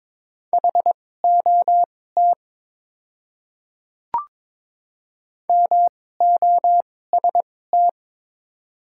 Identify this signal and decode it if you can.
Morse code, heard as a keyed tone, about 22 wpm, spaced as for 16 wpm, 700 Hz. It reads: HOT MOST
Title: 2x Character Spacing - Top 100 Words - 22wpm